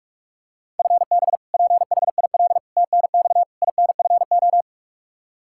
Morse code, heard as a keyed tone, about 46 wpm, 700 Hz. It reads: FB PHIL TNX INFO
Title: Practice QSO - Exchange 3 - More Info - 45wpm